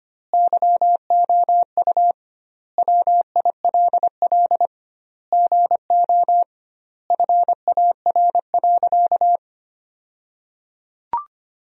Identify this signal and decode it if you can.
Morse code, heard as a keyed tone, about 25 wpm, 700 Hz. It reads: YOU WILL GO FAR.
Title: 2x Word-Spacing - Sentences from Top 400 Words - 25wpm